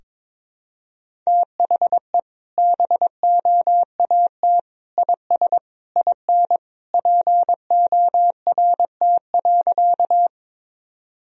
Morse code, heard as a keyed tone, about 22 wpm, 700 Hz. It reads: THE BOAT IS IN PORT.
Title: Sentences from Top 1000 Words - Review Part 1 - 22wpm